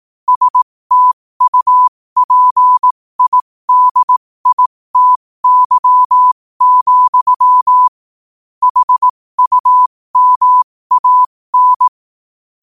Morse code, heard as a keyed tone, about 18 wpm, spaced as for 15 wpm, 1000 Hz. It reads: STUPIDITY, HUMAN
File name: qotdfiles / 15wpm